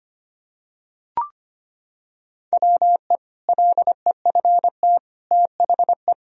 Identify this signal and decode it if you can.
Morse code, heard as a keyed tone, about 25 wpm, 700 Hz. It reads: WE LEFT THE
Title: Sentences from Top 1000 Words - Review Part 1 - 25wpm